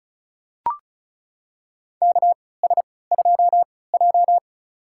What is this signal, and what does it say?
Morse code, heard as a keyed tone, about 36 wpm, spaced as for 20 wpm, 700 Hz. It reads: KS2J
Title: USA Call Signs - 4 Characters - 3x Character Spacing - 35wpm